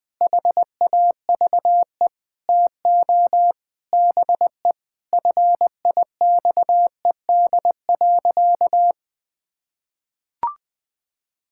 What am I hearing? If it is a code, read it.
Morse code, heard as a keyed tone, about 20 wpm, 700 Hz. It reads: HAVE TO BE FIXED.
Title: Sentences with 5-Word Phrases - Spoken Beginning - 20wpm